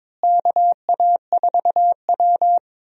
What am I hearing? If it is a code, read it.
Morse code, heard as a keyed tone, about 22 wpm, 700 Hz. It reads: KA4W
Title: USA Call Signs - 4 Characters - Repeated 3x - 22wpm